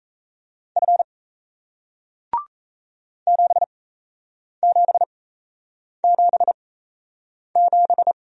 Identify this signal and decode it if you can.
Morse code, heard as a keyed tone, about 38 wpm, spaced as for 8 wpm, 700 Hz. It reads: F 7777